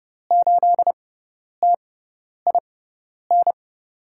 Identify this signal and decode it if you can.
Morse code, heard as a keyed tone, about 30 wpm, spaced as for 10 wpm, 700 Hz. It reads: 8TIN
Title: USA Call Signs - 6 Characters - 6x Character Spacing - 30wpm